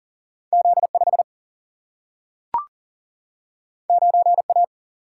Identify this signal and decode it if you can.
Morse code, heard as a keyed tone, about 40 wpm, 700 Hz. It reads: Z 5 9 A
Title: Single-Letter Single-Number 40wpm